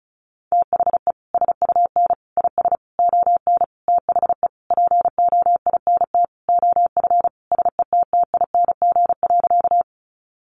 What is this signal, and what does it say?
Morse code, heard as a keyed tone, about 34 wpm, 700 Hz. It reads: THE SUN IS ON THE POINT OF SETTING.
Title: Sentences from Top 1300 Words 35wpm